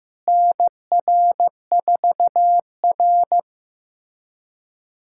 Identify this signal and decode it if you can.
Morse code, heard as a keyed tone, about 15 wpm, 700 Hz. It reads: NR4R
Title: USA Callsign DE Callsign - 4 Characters - 15wpm